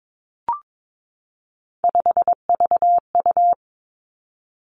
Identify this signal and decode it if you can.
Morse code, heard as a keyed tone, about 22 wpm, 700 Hz. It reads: 5VU
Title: Three Challenging Letter-Number Combinations 22wpm